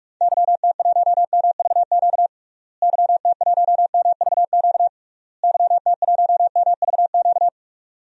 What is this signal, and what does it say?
Morse code, heard as a keyed tone, about 41 wpm, 700 Hz. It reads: YT1MVQ YT1MVQ YT1MVQ
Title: DXCC-100 Call Signs - 6 Characters - Repeated 3x - 45wpm